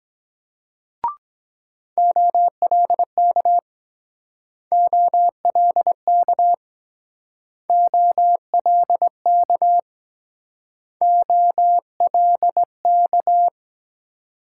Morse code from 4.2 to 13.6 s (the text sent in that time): OLK OLK OLK